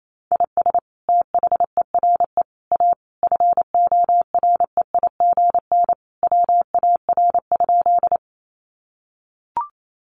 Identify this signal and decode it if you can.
Morse code, heard as a keyed tone, about 28 wpm, 700 Hz. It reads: IS THERE A FOREIGN WAR?
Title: Sentences from Top 1400 Words 28wpm